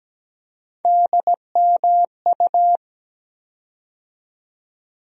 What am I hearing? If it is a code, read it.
Morse code, heard as a keyed tone, about 17 wpm, 700 Hz. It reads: DMU
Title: USA Call Sign Postfixes - 3 Letters 17wpm